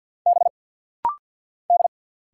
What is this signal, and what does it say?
Morse code, heard as a keyed tone, about 50 wpm, 700 Hz. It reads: B D
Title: Single Letter-Number Alt - Rapid-Fire 50wpm